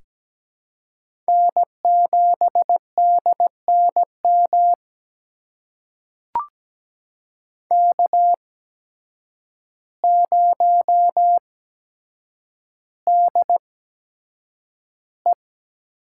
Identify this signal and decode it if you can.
Morse code, heard as a keyed tone, about 17 wpm, spaced as for 8 wpm, 700 Hz. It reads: N7DNM K0DE
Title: USA Call Signs - 5 Characters - 8x Character Spacing - 17wpm